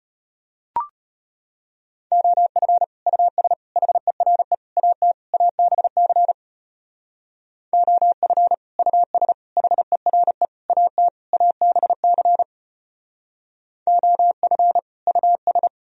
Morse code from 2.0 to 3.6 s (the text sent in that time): OF US